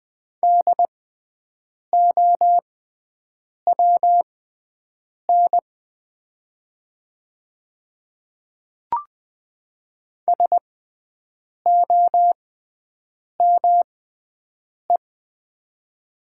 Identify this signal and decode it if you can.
Morse code, heard as a keyed tone, about 20 wpm, spaced as for 7 wpm, 700 Hz. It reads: DOWN SOME
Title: 6x Character Spacing - Top 100 Words - 20wpm